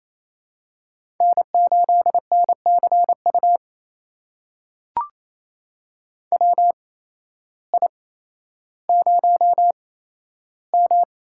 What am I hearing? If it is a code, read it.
Morse code, heard as a keyed tone, about 28 wpm, spaced as for 13 wpm, 700 Hz. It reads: N8NCU WI0M
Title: USA Call Signs - 8x Character Spacing - 28wpm